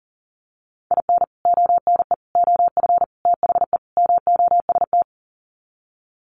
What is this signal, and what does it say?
Morse code, heard as a keyed tone, about 40 wpm, 700 Hz. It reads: IN ONE OF THE MOST
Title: Sets of 5 Words Encore 40wpm